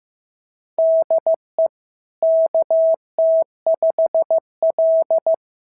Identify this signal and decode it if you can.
Morse code, heard as a keyed tone, about 14 wpm, 650 Hz. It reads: DE KT5L
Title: SOTA QSO Practice 15wpm